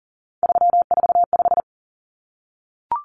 Morse code, heard as a keyed tone, about 40 wpm, 700 Hz. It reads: 345
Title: Numbers - 3-Digit 40wpm